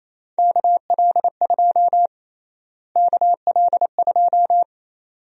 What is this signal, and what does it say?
Morse code, heard as a keyed tone, about 28 wpm, 700 Hz. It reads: KL2 KL2
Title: USA Call Sign Prefixes - Repeated 2x - 28wpm